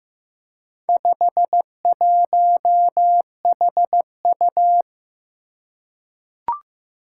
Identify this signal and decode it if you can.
Morse code, heard as a keyed tone, about 15 wpm, 700 Hz. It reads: 51HU
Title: Four Challenging Letter-Number Combinations 15wpm